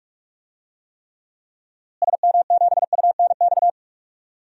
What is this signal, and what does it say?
Morse code, heard as a keyed tone, about 45 wpm, 700 Hz. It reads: SM7UNX
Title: DXCC-100 Call Signs - 6 Characters - Repeated 3x - 45wpm